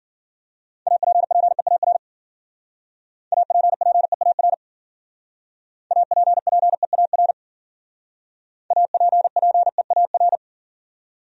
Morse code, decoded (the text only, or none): APPEAR APPEAR APPEAR APPEAR